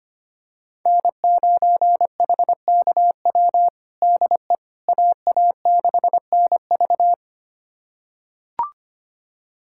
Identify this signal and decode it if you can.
Morse code, heard as a keyed tone, about 25 wpm, 700 Hz. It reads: N9HKW DE AA6NV
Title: USA Callsign DE Callsign - 5 Characters - 25wpm